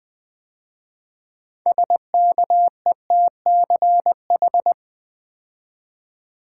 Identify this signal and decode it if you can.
Morse code, heard as a keyed tone, about 20 wpm, 700 Hz. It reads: SKETCH
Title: 6 Letter Words - First 3 Letters Given - 20wpm